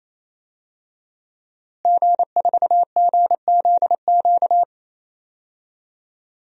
Morse code, heard as a keyed tone, about 28 wpm, 700 Hz. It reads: G4GZQ
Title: DXCC-100 Call Signs - 5 Characters - Repeated 3x - 28wpm